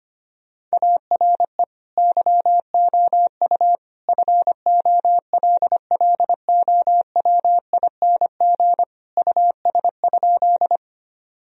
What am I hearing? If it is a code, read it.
Morse code, heard as a keyed tone, about 25 wpm, 700 Hz. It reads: ARE YOU FOLLOWING US?
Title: Sentences from Top 1000 Words - Review Part 4 - 25wpm